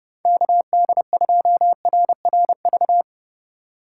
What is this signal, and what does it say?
Morse code, heard as a keyed tone, about 32 wpm, 700 Hz. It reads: KD2RRV
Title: USA Call Signs - 6 Characters - 4x Character Spacing - 30wpm